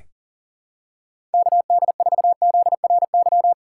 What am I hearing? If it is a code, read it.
Morse code, heard as a keyed tone, about 40 wpm, 700 Hz. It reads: KD4ZRY
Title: USA Call Signs - 4x Character Spacing - 40wpm